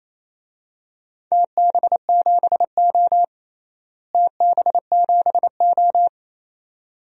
Morse code, heard as a keyed tone, about 27 wpm, 700 Hz. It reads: TB7O TB7O
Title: DXCC-100 Call Signs - 4 Characters - Repeated 2x - 28wpm